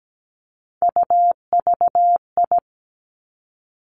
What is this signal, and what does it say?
Morse code, heard as a keyed tone, about 17 wpm, 700 Hz. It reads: UVI
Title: Three Challenging Letter-Number Combinations 17wpm